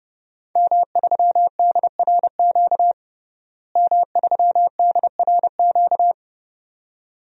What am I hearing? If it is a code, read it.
Morse code, heard as a keyed tone, about 30 wpm, 700 Hz. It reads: M3DRQ M3DRQ
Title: DXCC-100 Call Signs - 5 Characters - Repeated 2x - 30wpm